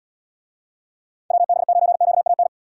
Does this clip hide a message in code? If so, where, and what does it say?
Yes: Morse code from 1.3 to 2.5 s, heard as a keyed tone, about 78 wpm, 700 Hz. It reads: WB9QAN